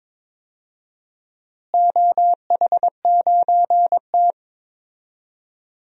Morse code, heard as a keyed tone, about 22 wpm, 700 Hz. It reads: OH9T
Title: DXCC-100 Call Signs - 4 Characters - Repeated 2x - 22wpm